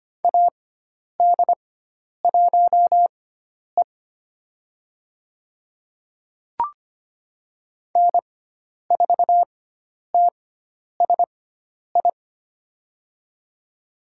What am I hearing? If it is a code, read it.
Morse code, heard as a keyed tone, about 25 wpm, spaced as for 10 wpm, 700 Hz. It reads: AD1E N4TSI
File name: Lesson 16g - Letter c - Call Signs 25wpm